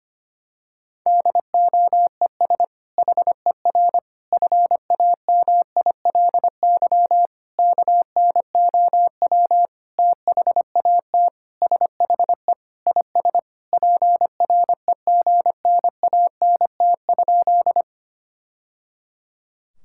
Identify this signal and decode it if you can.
Morse code, heard as a keyed tone, about 25 wpm, 700 Hz. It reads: DOES HER FAMILY KNOW THAT SHE IS PREGNANT?
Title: Sentences from Top 1000 Words - Review Part 6 - 25wpm